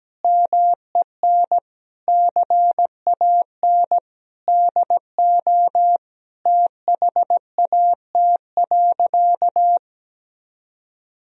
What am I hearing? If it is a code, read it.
Morse code, heard as a keyed tone, about 17 wpm, 700 Hz. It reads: MEN CAN DO THAT.
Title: Sentences from Top 1000 Words - Review Part 1 - 17wpm